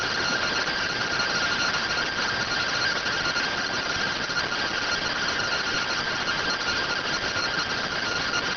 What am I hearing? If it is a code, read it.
Two-tone teleprinter signal, about 170 Hz shift.